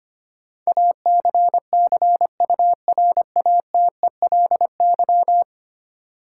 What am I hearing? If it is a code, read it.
Morse code, heard as a keyed tone, about 25 wpm, 700 Hz. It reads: ACCURATELY
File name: 10 Letter Words - First 3 Letters Given - 25wpm